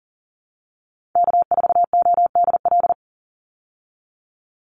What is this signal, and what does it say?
Morse code, heard as a keyed tone, about 40 wpm, 700 Hz. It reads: K4ODL
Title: Call Signs Repeated 3x - 40wpm